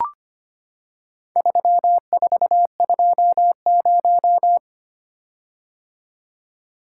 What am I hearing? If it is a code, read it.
Morse code, heard as a keyed tone, about 25 wpm, 700 Hz. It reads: 3420
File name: practice / 25wpm / Four Letter-Number Combinations - 25wpm